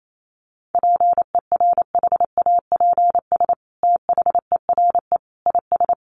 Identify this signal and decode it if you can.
Morse code, heard as a keyed tone, about 28 wpm, 700 Hz. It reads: PERHAPS THERE IS
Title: Sentences from Top 600 Words 28wpm